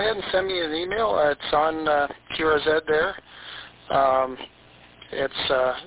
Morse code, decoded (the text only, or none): none